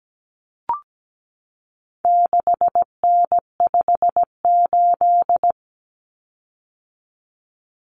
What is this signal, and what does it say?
Morse code, heard as a keyed tone, about 17 wpm, 700 Hz. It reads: 6N58
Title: Common Cut Numbers - 4-Digit 17wpm